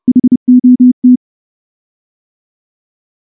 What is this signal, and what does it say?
Morse code, heard as a keyed tone, about 29 wpm, 250 Hz. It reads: HOT
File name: morse / morse-training-30WPM